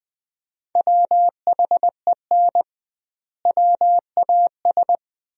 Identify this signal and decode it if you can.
Morse code, heard as a keyed tone, about 20 wpm, 700 Hz. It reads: WHEN WAS
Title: 2x Word-Spacing - Sentences from Top 100 Words - 20wpm